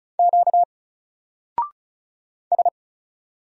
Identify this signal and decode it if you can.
Morse code, heard as a keyed tone, about 36 wpm, 700 Hz. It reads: Q S
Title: Single Character (Letter:Number:Punctuation:Prosign) Alt 35wpm